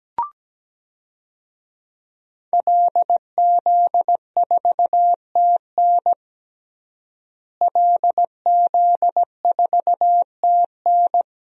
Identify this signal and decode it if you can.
Morse code, heard as a keyed tone, about 17 wpm, 700 Hz. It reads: LZ4TN LZ4TN
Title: DXCC-100 Call Signs - 5 Characters - Repeated 3x - 17wpm